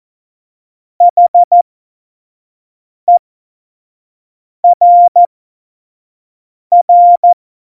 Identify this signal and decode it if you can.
Morse code, heard as a keyed tone, about 14 wpm, spaced as for 5 wpm, 700 Hz. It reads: HERR
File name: Morsekurs / Witze / Witz 24. September 2019, 25 F70